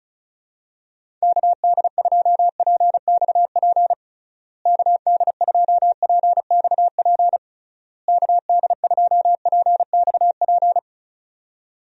Morse code, heard as a keyed tone, about 36 wpm, 700 Hz. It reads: KD2PXP KD2PXP KD2PXP
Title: USA Call Signs - 6 Characters - Repeated 3x - 35wpm